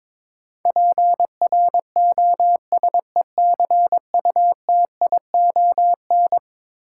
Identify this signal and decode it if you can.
Morse code, heard as a keyed tone, about 22 wpm, 700 Hz. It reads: PROSECUTION
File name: practice / 22wpm / Eleven Letter Words - 22wpm